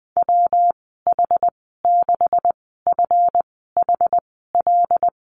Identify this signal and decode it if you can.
Morse code, heard as a keyed tone, about 20 wpm, spaced as for 14 wpm, 700 Hz. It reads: WH6FHL
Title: 161 Lesson 26g - Letter g - Call Signs 20wpm